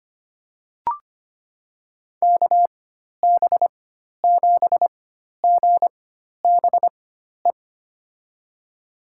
Morse code, heard as a keyed tone, about 25 wpm, spaced as for 12 wpm, 700 Hz. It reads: KB7GBE
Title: USA Call Signs - 6 Characters - 4x Character Spacing - 25wpm